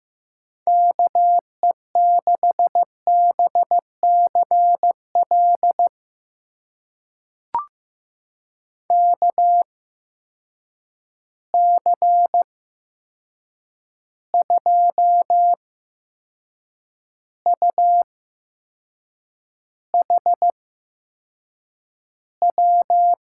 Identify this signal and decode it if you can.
Morse code, heard as a keyed tone, about 15 wpm, spaced as for 6 wpm, 700 Hz. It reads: KE6BCL KC2UHW
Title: USA Call Signs - 6 Characters - 8x Character Spacing - 15wpm